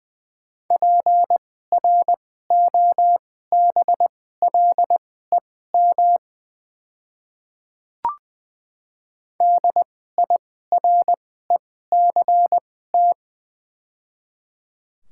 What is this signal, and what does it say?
Morse code, heard as a keyed tone, about 21 wpm, spaced as for 15 wpm, 700 Hz. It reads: PROBLEM DIRECT
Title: 2x Character Spacing - Top 500 Words - 20wpm